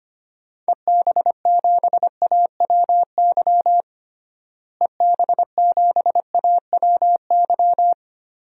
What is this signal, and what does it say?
Morse code, heard as a keyed tone, about 25 wpm, 700 Hz. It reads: EB7AWY EB7AWY
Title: DXCC-100 Call Signs - 6 Characters - Repeated 2x - 25wpm